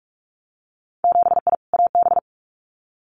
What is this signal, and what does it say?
Morse code, heard as a keyed tone, about 46 wpm, 700 Hz. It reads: 7I AB
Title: Winter Field Day Exchanges Encore 45wpm